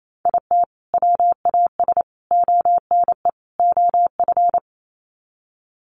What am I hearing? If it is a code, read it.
Morse code, heard as a keyed tone, about 28 wpm, 700 Hz. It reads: IT WAS ONE OF
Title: Sets of 4 Words 28wpm